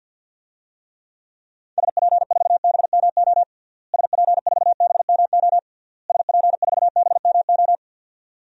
Morse code, heard as a keyed tone, about 52 wpm, 700 Hz. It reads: SP4BMO SP4BMO SP4BMO
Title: DXCC-100 Call Signs - 6 Characters - Repeated 3x - 50wpm